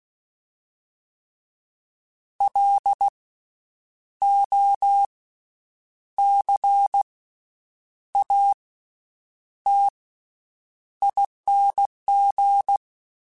Morse code, heard as a keyed tone, about 16 wpm, spaced as for 8 wpm, 800 Hz. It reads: LOCATING
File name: Special INGs / Special INGs-mp3s / VerySlowToFaster Ing @15